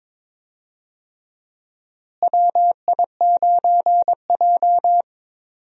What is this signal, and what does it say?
Morse code, heard as a keyed tone, about 22 wpm, 700 Hz. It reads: WI9J
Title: USA Call Signs - 4 Characters - Repeated 3x - 22wpm